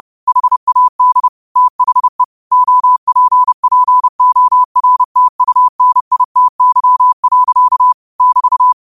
Morse code, heard as a keyed tone, about 30 wpm, 1000 Hz. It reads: HAD THE OPPORTUNITY. <BT>